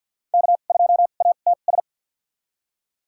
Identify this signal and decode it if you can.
Morse code, heard as a keyed tone, about 47 wpm, spaced as for 35 wpm, 700 Hz. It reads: K2ATS
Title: USA Call Signs - 5 Characters - 2x Character Spacing - 50wpm